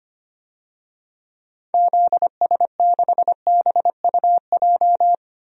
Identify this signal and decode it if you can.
Morse code, heard as a keyed tone, about 25 wpm, 700 Hz. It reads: ZS6BUJ
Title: DXCC-100 Call Signs - 6 Characters - Repeated 3x - 25wpm